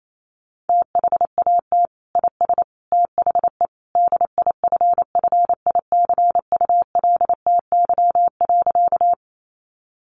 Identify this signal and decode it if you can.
Morse code, heard as a keyed tone, about 28 wpm, 700 Hz. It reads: THAT IS THE DIFFICULTY.
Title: Sentences from Top 1500 Words 28wpm